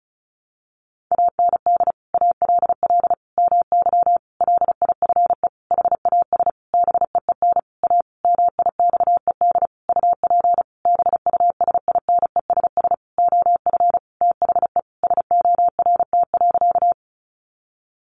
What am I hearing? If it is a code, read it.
Morse code, heard as a keyed tone, about 34 wpm, 700 Hz. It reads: AND ALL MY LIFE HAS BEEN A MIXED UP BUSINESS OF THE SORT.
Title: Sentences from Top 1500 Words 35wpm